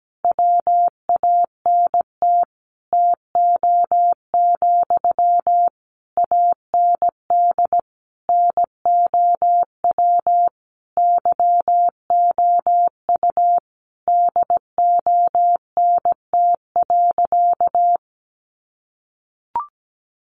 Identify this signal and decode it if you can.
Morse code, heard as a keyed tone, about 17 wpm, 700 Hz. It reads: WANT TO, AND NOW YOU DONT.
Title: Sentences from Top 600 Words 17wpm